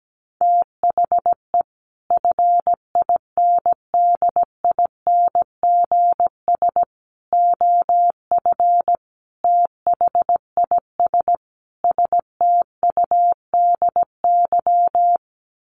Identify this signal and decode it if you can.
Morse code, heard as a keyed tone, about 17 wpm, 700 Hz. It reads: THE FINDINGS OF THIS STUDY
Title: Sets of 5 Words Encore2 17wpm